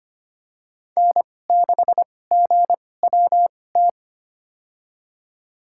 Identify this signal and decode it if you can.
Morse code, heard as a keyed tone, about 25 wpm, spaced as for 18 wpm, 700 Hz. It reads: N6GWT
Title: USA Call Signs - 5 Characters - 2x Character Spacing - 25wpm